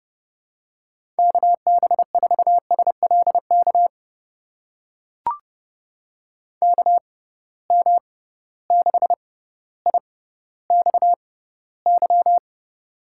Morse code, heard as a keyed tone, about 30 wpm, spaced as for 15 wpm, 700 Hz. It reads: KB4SLK KM6IXY
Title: USA Call Signs - 6 Characters - 6x Character Spacing - 30wpm